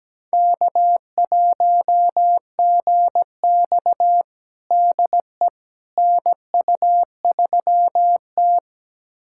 Morse code, heard as a keyed tone, about 17 wpm, 700 Hz. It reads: K1GX DE NU3T
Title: USA Callsign DE Callsign - 4 Characters - 17wpm